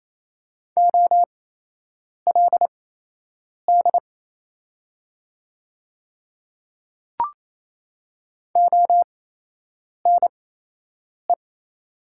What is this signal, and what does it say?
Morse code, heard as a keyed tone, about 28 wpm, spaced as for 8 wpm, 700 Hz. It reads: OLD ONE
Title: 8x Character Spacing - Top 200 Words - 28wpm